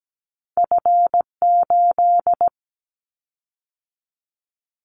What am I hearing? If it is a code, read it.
Morse code, heard as a keyed tone, about 17 wpm, 700 Hz. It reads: F8
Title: Callsign Prefixes for Top 25 DXCC - 17wpm